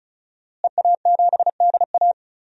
Morse code, heard as a keyed tone, about 34 wpm, 700 Hz. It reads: EA7DA
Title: DXCC-100 Call Signs - 5 Characters - Repeated 3x - 35wpm